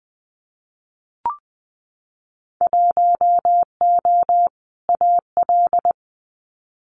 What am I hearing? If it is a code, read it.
Morse code, heard as a keyed tone, about 20 wpm, 700 Hz. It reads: 1O AL
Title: Winter Field Day Exchanges 20wpm